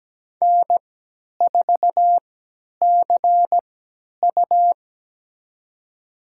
Morse code, heard as a keyed tone, about 17 wpm, spaced as for 10 wpm, 700 Hz. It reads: N4CU
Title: USA Call Signs - 3x Character Spacing - 17wpm